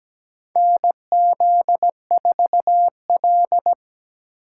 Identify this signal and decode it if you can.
Morse code, heard as a keyed tone, about 17 wpm, 700 Hz. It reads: NZ4L
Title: USA Call Signs - 2x Character Spacing - 17wpm